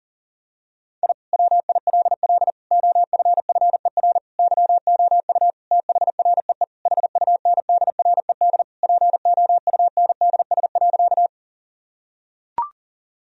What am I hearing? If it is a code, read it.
Morse code, heard as a keyed tone, about 37 wpm, 700 Hz. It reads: I WILL OFFER YOU THREE HUNDRED POUNDS.
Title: Sentences from Top 1000 Words - Review Part 4 - 40wpm